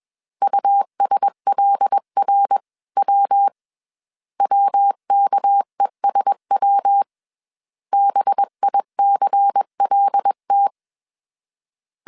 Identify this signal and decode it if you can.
Morse code, heard as a keyed tone, about 21 wpm, 800 Hz. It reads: USLRW WKESW BICLT
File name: CW / DET / 1 6 1500C S C 20wpm 2116